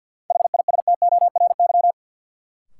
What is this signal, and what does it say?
Morse code, heard as a keyed tone, about 52 wpm, 700 Hz. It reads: HISTORY
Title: Sentences from Top 600 Words - New Word Drill - 50wpm